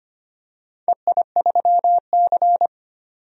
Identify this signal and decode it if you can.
Morse code, heard as a keyed tone, about 25 wpm, 700 Hz. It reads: EI3C
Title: DXCC-100 Call Signs - 4 Characters - 25wpm